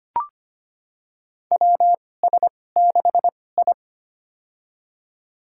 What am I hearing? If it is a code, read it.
Morse code, heard as a keyed tone, about 25 wpm, spaced as for 18 wpm, 700 Hz. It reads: WS6I
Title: USA Call Signs - 4 Characters - 2x Character Spacing - 25wpm